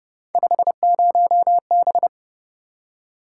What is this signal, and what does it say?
Morse code, heard as a keyed tone, about 30 wpm, 700 Hz. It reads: 50B